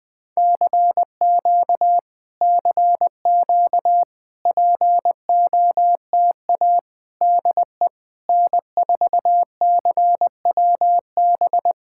Morse code, heard as a keyed tone, about 20 wpm, 700 Hz. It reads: CQ CQ POTA DE N4CWB
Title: POTA QSO Practice - 20wpm